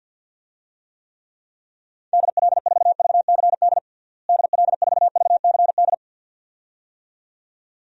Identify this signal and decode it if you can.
Morse code, heard as a keyed tone, about 52 wpm, 700 Hz. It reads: DL4VCD DL4VCD
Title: DXCC-100 Call Signs - 6 Characters - Repeated 2x - 50wpm